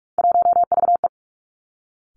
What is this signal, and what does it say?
Morse code, heard as a keyed tone, about 46 wpm, 700 Hz. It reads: 1VE
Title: Three Challenging Letter-Number Combinations 45wpm